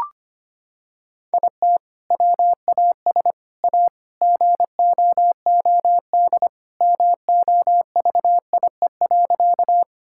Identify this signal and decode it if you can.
Morse code, heard as a keyed tone, about 25 wpm, 700 Hz. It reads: IT WAS A GOOD MOVIE.
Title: Sentences from Top 1000 Words - Review Part 5 - 25wpm